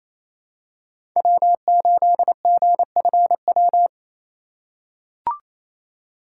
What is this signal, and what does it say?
Morse code, heard as a keyed tone, about 28 wpm, 700 Hz. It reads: W8GFW
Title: USA Call Signs - 5 Characters - 4x Character Spacing - 28wpm